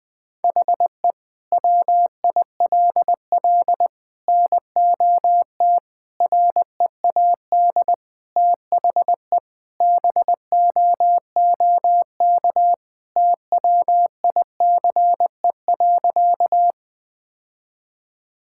Morse code, heard as a keyed tone, about 20 wpm, 700 Hz. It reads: HE WILL NOT READ THE BOOK TWICE.
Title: Sentences from Top 1600 Words 20wpm